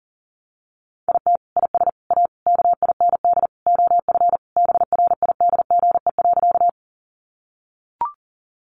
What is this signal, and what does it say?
Morse code, heard as a keyed tone, about 40 wpm, 700 Hz. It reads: IT IS A KIND OF BRIDGE.
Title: Sentences from Top 1200 Words 40wpm